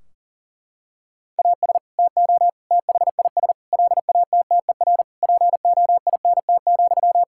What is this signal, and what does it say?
Morse code, heard as a keyed tone, about 40 wpm, 700 Hz. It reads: AS TO THIS LATTER POINT,
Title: Sentences from Top 1600 Words 40wpm